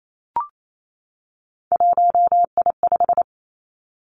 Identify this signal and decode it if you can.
Morse code, heard as a keyed tone, about 28 wpm, 700 Hz. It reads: 1I5
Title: Three Challenging Letter-Number Combinations 28wpm